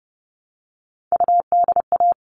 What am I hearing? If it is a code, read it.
Morse code, heard as a keyed tone, about 30 wpm, 700 Hz. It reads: UDA
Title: All Cut Numbers - 3-Digit 30wpm